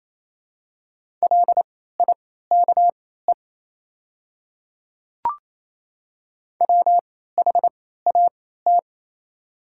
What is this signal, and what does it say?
Morse code, heard as a keyed tone, about 28 wpm, spaced as for 16 wpm, 700 Hz. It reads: LIKE WHAT
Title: 3x Character Spacing - Top 200 Words - 28wpm